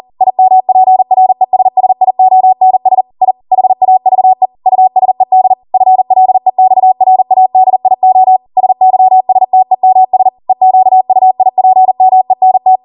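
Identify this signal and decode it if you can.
Morse code, heard as a keyed tone, about 40 wpm, 750 Hz. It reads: IMPRESSIONS I HAVE USED FLEXRADIO SYSTEMS EQUIPMENT
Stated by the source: Code practice at 40 wpm.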